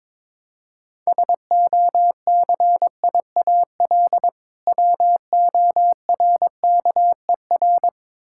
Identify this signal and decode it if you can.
Morse code, heard as a keyed tone, about 21 wpm, 700 Hz. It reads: SOCIAL WORKER
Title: Practice QSO - Exchange 3 - More Info - 22wpm